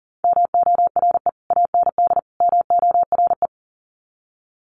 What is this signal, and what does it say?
Morse code, heard as a keyed tone, about 38 wpm, 700 Hz. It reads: MORE AND MORE
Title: Sets of 3 Words 40wpm